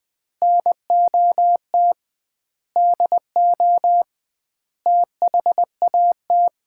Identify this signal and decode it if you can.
Morse code, heard as a keyed tone, about 20 wpm, 700 Hz. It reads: NOT DO THAT
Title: 2x Word-Spacing - Sets of 5 Words - 20wpm